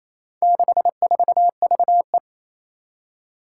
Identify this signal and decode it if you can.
Morse code, heard as a keyed tone, about 28 wpm, 700 Hz. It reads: 64VE
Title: Four Challenging Letter-Number Combinations 28wpm